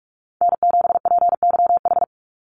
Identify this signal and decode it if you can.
Morse code, heard as a keyed tone, about 44 wpm, 700 Hz. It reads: N7PYH
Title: USA Call Signs 45wpm